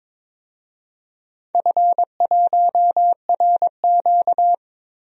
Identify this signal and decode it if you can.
Morse code, heard as a keyed tone, about 21 wpm, 700 Hz. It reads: F1RQ
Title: DXCC-100 Call Signs - 4 Characters - Repeated 2x - 22wpm